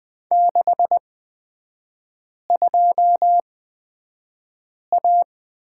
Morse code, heard as a keyed tone, about 20 wpm, spaced as for 5 wpm, 700 Hz. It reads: 62A
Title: 255 Lesson 41e - Period - Three Characters 20wpm